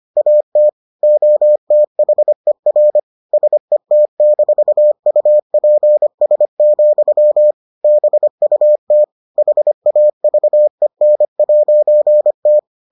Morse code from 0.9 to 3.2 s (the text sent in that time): OTHER